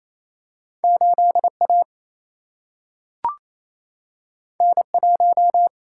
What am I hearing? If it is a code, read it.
Morse code, heard as a keyed tone, about 28 wpm, 700 Hz. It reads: 8 A N 1
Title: Single-Letter Single-Number 28wpm